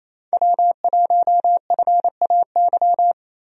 Morse code, heard as a keyed tone, about 27 wpm, 700 Hz. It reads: W1FAY
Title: USA Call Signs - 5 Characters - Repeated 2x - 28wpm